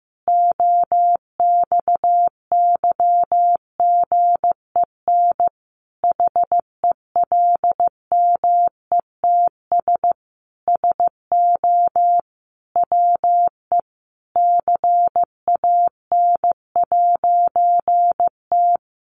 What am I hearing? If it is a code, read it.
Morse code, heard as a keyed tone, about 15 wpm, 700 Hz. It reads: OXYGEN HELMETS SO WE CAN'T
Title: A Sound Of Thunder 15wpm